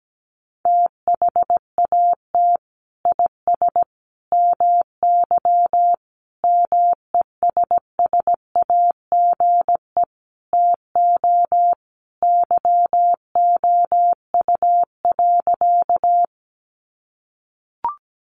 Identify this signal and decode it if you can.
Morse code, heard as a keyed tone, about 17 wpm, 700 Hz. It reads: THAT IS MY MESSAGE TO YOU.
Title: Sentences from Top 1100 Words 17wpm